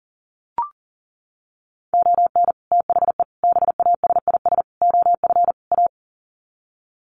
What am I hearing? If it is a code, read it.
Morse code, heard as a keyed tone, about 40 wpm, 700 Hz. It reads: ON THE BASIS OF A
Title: Sets of 5 Words Encore 40wpm